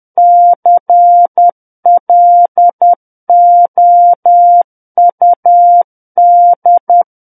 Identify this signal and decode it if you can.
Morse code, heard as a keyed tone, about 10 wpm, 700 Hz. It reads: CLOUD
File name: Book One / 01-Chapter 1 - 10 WPM